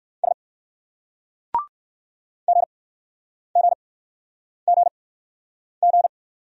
Morse code, heard as a keyed tone, about 59 wpm, spaced as for 9 wpm, 700 Hz. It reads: S GGGG